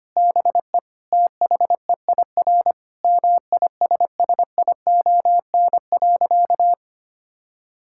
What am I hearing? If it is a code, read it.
Morse code, heard as a keyed tone, about 25 wpm, 700 Hz. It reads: BE THEIR MISSION.
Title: Sentences from Top 1600 Words 25wpm